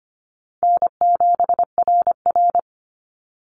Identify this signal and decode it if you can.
Morse code, heard as a keyed tone, about 25 wpm, 700 Hz. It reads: N7RR
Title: NA SOTA Activator Callsigns 25wpm